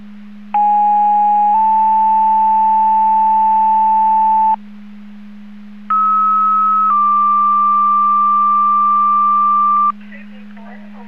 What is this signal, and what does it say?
Two-tone page, heard as a one-second tone, then a three-second tone, about 830 then 880 Hz, about 1290 then 1180 Hz, with speech. It is heard from 0.5 to 9.9 s.